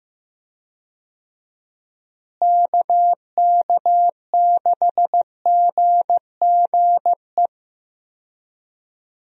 Morse code, heard as a keyed tone, about 15 wpm, 700 Hz. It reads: KK6GGE
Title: USA Call Signs - 6 Characters - Repeated 2x - 15wpm